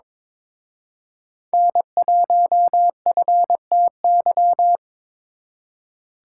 Morse code, heard as a keyed tone, about 26 wpm, 700 Hz. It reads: N1FTY
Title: Speed Racing - USA 1x3 Call Signs 22wpm